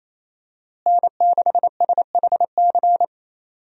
Morse code, heard as a keyed tone, about 28 wpm, 700 Hz. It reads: N6SHC
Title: USA Call Signs - 5 Characters - Repeated 3x - 28wpm